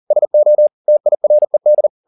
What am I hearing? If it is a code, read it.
Morse code, heard as a keyed tone, about 40 wpm, 600 Hz. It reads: SO TIRED